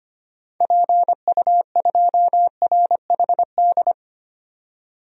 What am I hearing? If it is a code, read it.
Morse code, heard as a keyed tone, about 25 wpm, 700 Hz. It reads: PU2RHD
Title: DXCC-100 Call Signs - 6 Characters - 25wpm